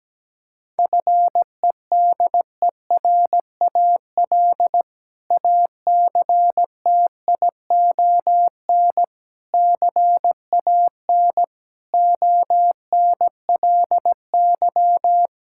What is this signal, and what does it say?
Morse code, heard as a keyed tone, about 17 wpm, 700 Hz. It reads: FEDERAL ACTION CAN ONLY
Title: Sentences from Top 1000 Words - Review Part 5 - 17wpm